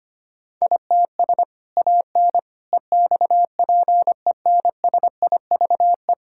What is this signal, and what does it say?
Morse code, heard as a keyed tone, about 25 wpm, 700 Hz. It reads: ITS AN EXPENSIVE
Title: Sentences from Top 1000 Words - Review Part 6 - 25wpm